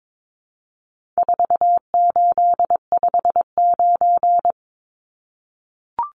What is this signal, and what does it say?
Morse code, heard as a keyed tone, about 22 wpm, 700 Hz. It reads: 4859
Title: Numbers - 4-Digit 22wpm